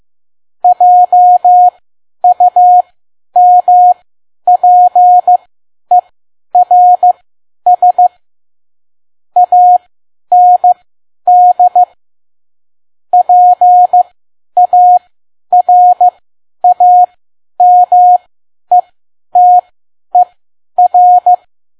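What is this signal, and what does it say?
Morse code, heard as a keyed tone, about 15 wpm, spaced as for 10 wpm, 720 Hz. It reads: JUMPERS AND PARAMETER
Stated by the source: Morse code practice at 10 WPM.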